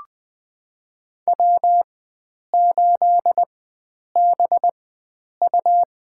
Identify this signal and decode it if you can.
Morse code, heard as a keyed tone, about 20 wpm, spaced as for 9 wpm, 700 Hz. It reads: W8BU
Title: USA Call Signs - 4x Character Spacing - 20wpm